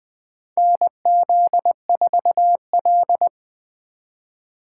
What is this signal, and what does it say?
Morse code, heard as a keyed tone, about 20 wpm, 700 Hz. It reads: NZ4L
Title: USA Call Signs - 2x Character Spacing - 20wpm